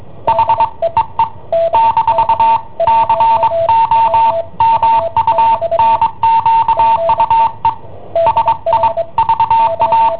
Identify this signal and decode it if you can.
Morse code, heard as a keyed tone, about 22 wpm, 910 Hz, with noise. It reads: H EE - COMUNQUE SI VA